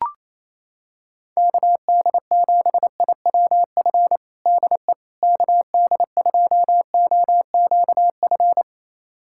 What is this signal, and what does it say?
Morse code, heard as a keyed tone, about 28 wpm, 700 Hz. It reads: KD7IWF DE KD2OQF
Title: USA Callsign DE Callsign - 6 Characters - 28wpm